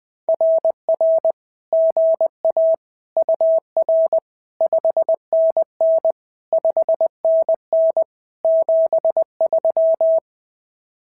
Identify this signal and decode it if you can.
Morse code, heard as a keyed tone, about 20 wpm, 650 Hz. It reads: RR GA UR 5NN 5NN 73
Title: SOTA QSO Practice 20wpm